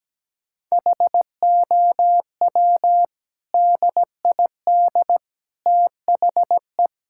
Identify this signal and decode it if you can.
Morse code, heard as a keyed tone, about 17 wpm, 700 Hz. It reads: HOW DID THE
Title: Sentences from Top 1000 Words - Review Part 6 - 17wpm